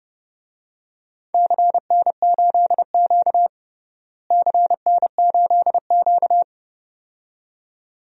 Morse code, heard as a keyed tone, about 30 wpm, 700 Hz. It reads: CN8Q CN8Q
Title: DXCC-100 Call Signs - 4 Characters - Repeated 2x - 30wpm